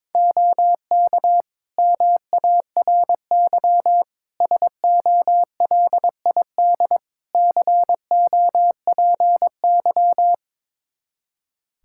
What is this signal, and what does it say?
Morse code, heard as a keyed tone, about 22 wpm, 700 Hz. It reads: OK MARY SOLID COPY
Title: Practice QSO - Exchange 3 - More Info - 22wpm